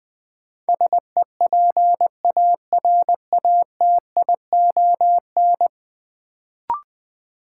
Morse code, heard as a keyed tone, about 20 wpm, 700 Hz. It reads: SEPARATION
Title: Sets of 5 Words - New Word Drill - 20wpm